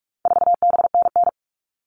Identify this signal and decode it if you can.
Morse code, heard as a keyed tone, about 44 wpm, 700 Hz. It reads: 4BNN
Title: All Cut Numbers - 4-Digit 45wpm